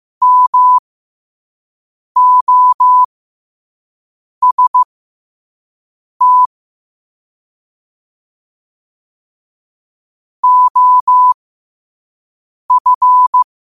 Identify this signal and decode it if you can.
Morse code, heard as a keyed tone, about 15 wpm, spaced as for 5 wpm, 1000 Hz. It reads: MOST OF
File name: qotdfiles / 5wpm